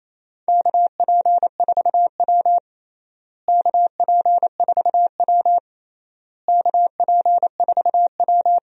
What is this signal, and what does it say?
Morse code, heard as a keyed tone, about 28 wpm, 700 Hz. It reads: KP4W KP4W KP4W
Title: DXCC-100 Call Signs - 4 Characters - Repeated 3x - 28wpm